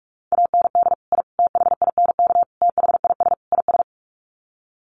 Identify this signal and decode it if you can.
Morse code, heard as a keyed tone, about 44 wpm, 700 Hz. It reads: AND I THINK THIS IS
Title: Sets of 5 Words Encore 45wpm